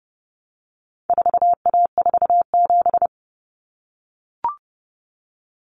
Morse code, heard as a keyed tone, about 30 wpm, 700 Hz. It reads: 4A47
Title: Common Cut Numbers - 4-Digit 30wpm